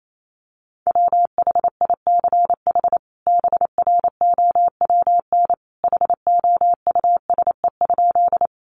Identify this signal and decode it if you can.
Morse code, heard as a keyed tone, about 28 wpm, 700 Hz. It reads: WHICH BROWN HOUSE?
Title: Sentences from Top 1200 Words 28wpm